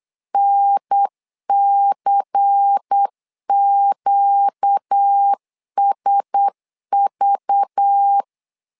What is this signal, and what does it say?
Morse code, heard as a keyed tone, about 8 wpm, 800 Hz. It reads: NCQSV
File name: LEZIONI / 5 LBVC / 5 LBVC 8wpm 2935